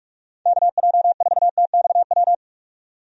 Morse code, heard as a keyed tone, about 46 wpm, 700 Hz. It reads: KJ4TXW
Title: USA Call Signs - 6 Characters - 45wpm